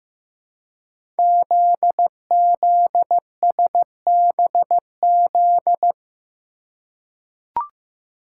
Morse code, heard as a keyed tone, about 15 wpm, 700 Hz. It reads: ZZSBZ
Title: Code Groups - 5 Letters - 15wpm